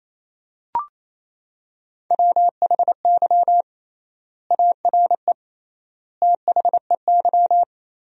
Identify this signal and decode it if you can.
Morse code, heard as a keyed tone, about 28 wpm, 700 Hz. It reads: WHY ARE THEY
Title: 3x Word-Spacing - Sentences from Top 200 Words - 28wpm